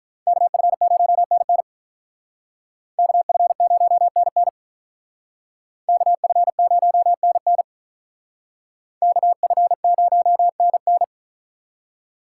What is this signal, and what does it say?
Morse code, heard as a keyed tone, about 45 wpm, 700 Hz. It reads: KF0NN KF0NN KF0NN KF0NN